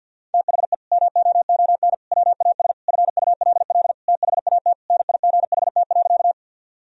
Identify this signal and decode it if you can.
Morse code, heard as a keyed tone, about 47 wpm, 700 Hz. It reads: THE MOON WAS FULL THAT NIGHT.
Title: Sentences from Top 1000 Words - Review Part 3 - 50wpm